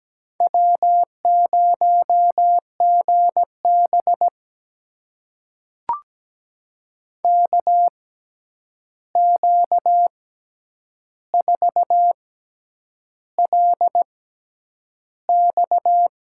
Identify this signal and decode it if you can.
Morse code, heard as a keyed tone, about 17 wpm, spaced as for 8 wpm, 700 Hz. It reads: W0GB KQ4LX
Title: USA Call Signs - 6x Character Spacing - 17wpm